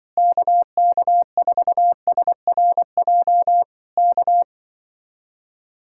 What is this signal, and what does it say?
Morse code, heard as a keyed tone, about 24 wpm, 700 Hz. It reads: KK4SRJ K
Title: Speed-Racing Common QSO Phrases - 20wpm